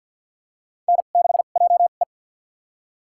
Morse code, heard as a keyed tone, about 52 wpm, spaced as for 36 wpm, 700 Hz. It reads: N6JE
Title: USA Call Signs - 4 Characters - 2x Character Spacing - 50wpm